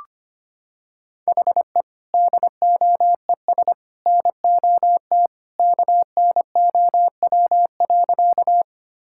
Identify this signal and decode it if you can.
Morse code, heard as a keyed tone, about 25 wpm, 700 Hz. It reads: HE DOES NOT KNOW.
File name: practice / 25wpm / Sentences with 3-Word Phrases - 25wpm